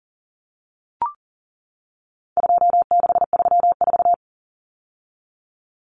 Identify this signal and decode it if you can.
Morse code, heard as a keyed tone, about 40 wpm, 700 Hz. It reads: 2634
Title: Numbers - 4-Digit 40wpm